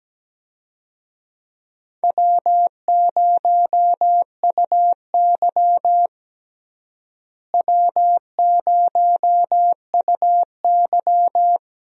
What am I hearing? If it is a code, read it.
Morse code, heard as a keyed tone, about 17 wpm, 700 Hz. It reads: W0UY W0UY
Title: USA Call Signs - 4 Characters - Repeated 3x - 17wpm